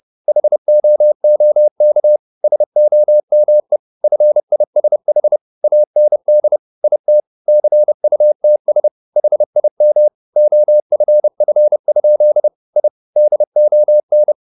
Morse code, heard as a keyed tone, about 30 wpm, 600 Hz. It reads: HOOK SOME FISH AND IT CUTS HIM OFF? I DON
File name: the old man and the sea / the old man and the sea 1 30wpm